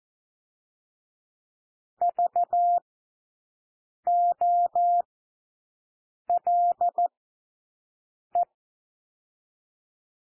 Morse code, heard as a keyed tone, about 14 wpm, spaced as for 6 wpm, 700 Hz. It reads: VOLE
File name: cwmp3 / W1 SOUND WORDS 14 wpm 0